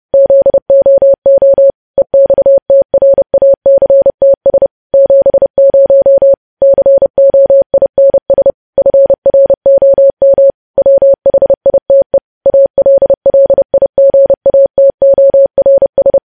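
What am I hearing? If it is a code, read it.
Morse code, heard as a keyed tone, about 30 wpm, 550 Hz. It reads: ZOO EXTRACTS 70 COINS FROM WHITE ALLIGATORS
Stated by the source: Morse code news headlines sent at 30.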